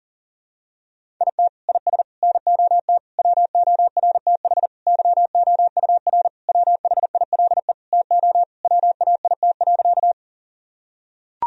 Morse code, heard as a keyed tone, about 40 wpm, 700 Hz. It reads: IT IS NOT WORTH YOUR WHILE TO WAIT.
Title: Sentences from Top 1000 Words - Review Part 1 - 40wpm